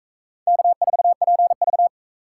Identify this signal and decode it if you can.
Morse code, heard as a keyed tone, about 41 wpm, 700 Hz. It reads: K4PV